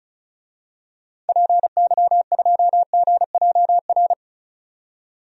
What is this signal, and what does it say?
Morse code, heard as a keyed tone, about 34 wpm, 700 Hz. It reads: PY2GJR
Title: DXCC-100 Call Signs - 6 Characters - 35wpm